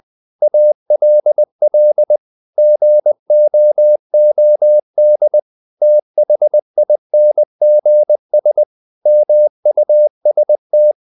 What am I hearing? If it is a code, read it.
Morse code, heard as a keyed tone, about 20 wpm, 600 Hz. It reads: ALL GOOD THINGS MUST
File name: CW / Practice / EnglishSayings20